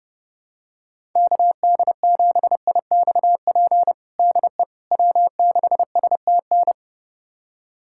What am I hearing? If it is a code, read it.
Morse code, heard as a keyed tone, about 30 wpm, 700 Hz. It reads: KD7IXP DE W6STN
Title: USA Callsign DE Callsign 30wpm